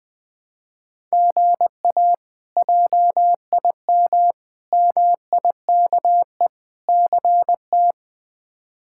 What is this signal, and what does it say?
Morse code, heard as a keyed tone, about 20 wpm, 700 Hz. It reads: GA JIM MIKE CT
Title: K1USN SST QSO Practice - 20wpm